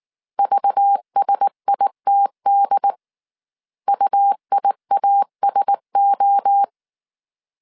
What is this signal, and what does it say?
Morse code, heard as a keyed tone, about 19 wpm, 800 Hz. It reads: VSITD UIASO
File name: CW / DET / 1 6 1500C S C 18wpm 2356